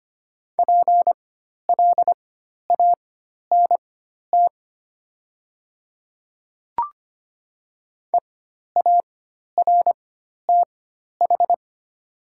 Morse code, heard as a keyed tone, about 25 wpm, spaced as for 12 wpm, 700 Hz. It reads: PLANT EARTH
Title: 4x Character Spacing - Top 300 Words - 25wpm